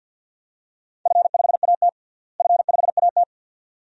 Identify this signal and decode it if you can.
Morse code, heard as a keyed tone, about 46 wpm, 700 Hz. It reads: F5AT F5AT
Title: DXCC-100 Call Signs - 4 Characters - Repeated 2x - 50wpm